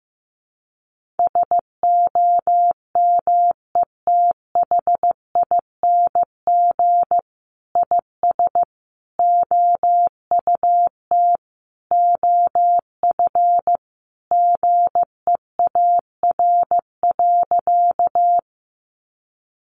Morse code, heard as a keyed tone, about 15 wpm, 700 Hz. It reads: SOMETHING IS OUT OF GEAR.
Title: Sentences from Top 1400 Words 15wpm